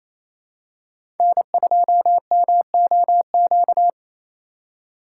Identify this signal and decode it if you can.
Morse code, heard as a keyed tone, about 28 wpm, 700 Hz. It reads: N2MOQ
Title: USA Call Signs - 5 Characters - 4x Character Spacing - 28wpm